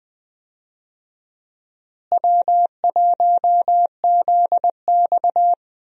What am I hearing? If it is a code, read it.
Morse code, heard as a keyed tone, about 20 wpm, 700 Hz. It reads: W1ZX
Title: USA Call Signs - 4 Characters - Repeated 2x - 20wpm